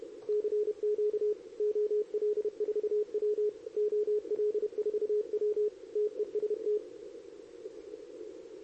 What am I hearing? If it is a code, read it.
Morse code, heard as a keyed tone, about 31 wpm, 420 Hz. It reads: CQ OL4W OL4W TEST
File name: CQWWCW2023 / RUN